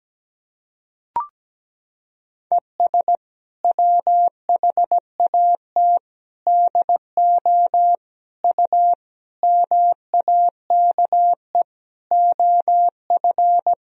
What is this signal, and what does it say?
Morse code, heard as a keyed tone, about 17 wpm, 700 Hz. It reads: ES WHAT DO U MAKE OF
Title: Sentences from Top 100 Words and Abbreviations 17wpm